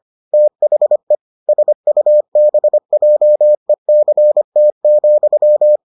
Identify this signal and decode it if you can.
Morse code, heard as a keyed tone, about 25 wpm, 600 Hz. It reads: THE SUBJECT,